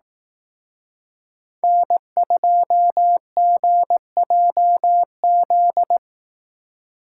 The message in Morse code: N2GJZ

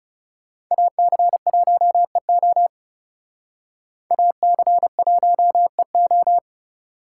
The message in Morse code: AC1EO AC1EO

